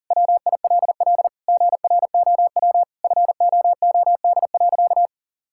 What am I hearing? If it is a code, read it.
Morse code, heard as a keyed tone, about 40 wpm, 700 Hz. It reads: WILL GROW FOOD.
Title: Sentences from Top 1000 Words - Review Part 6 - 40wpm